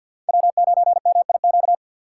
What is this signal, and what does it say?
Morse code, heard as a keyed tone, about 51 wpm, 700 Hz. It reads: W9MIQ
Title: USA Call Signs - 5 Characters - 3x Character Spacing - 50wpm